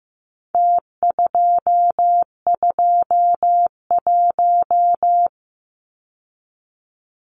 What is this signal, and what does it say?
Morse code, heard as a keyed tone, about 15 wpm, 700 Hz. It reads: T221
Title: Numbers - Contest Style - 4-Digit 15wpm